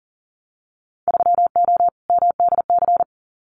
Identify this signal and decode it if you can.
Morse code, heard as a keyed tone, about 40 wpm, 700 Hz. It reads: 3O MDC
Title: Winter Field Day Exchanges Encore 40wpm